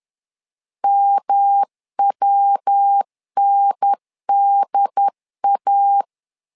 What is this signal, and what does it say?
Morse code, heard as a keyed tone, about 10 wpm, 800 Hz. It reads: MWNDA
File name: LEZIONI / 3 MWRO / 3 MW 10wpm 1959